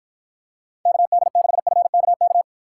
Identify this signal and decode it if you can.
Morse code, heard as a keyed tone, about 53 wpm, 700 Hz. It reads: KD6FKK